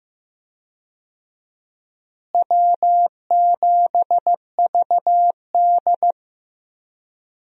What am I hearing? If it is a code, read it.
Morse code, heard as a keyed tone, about 15 wpm, 700 Hz. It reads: W7VD
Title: USA Call Signs - 4 Characters - Repeated 2x - 15wpm